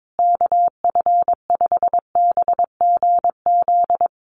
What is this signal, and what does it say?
Morse code, heard as a keyed tone, about 22 wpm, 700 Hz. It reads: KF5BGZ
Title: USA Call Signs Repeated 3x 22wpm